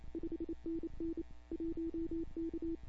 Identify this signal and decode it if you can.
Morse code, heard as a keyed tone, about 28 wpm, 340 Hz. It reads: 5NN 1K
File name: CW / 2005-k5tr arrl cw dx -15:00